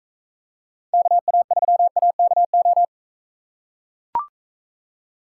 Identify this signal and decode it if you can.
Morse code, heard as a keyed tone, about 44 wpm, 700 Hz. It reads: KA3AKO